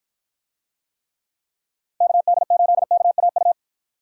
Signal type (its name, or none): Morse code